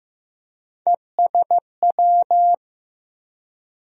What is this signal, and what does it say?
Morse code, heard as a keyed tone, about 15 wpm, 700 Hz. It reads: ESW
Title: USA Call Sign Postfixes - 3 Letters 15wpm